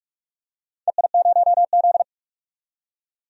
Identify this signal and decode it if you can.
Morse code, heard as a keyed tone, about 42 wpm, 700 Hz. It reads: EI0Z
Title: DXCC-100 Call Signs - 4 Characters - 45wpm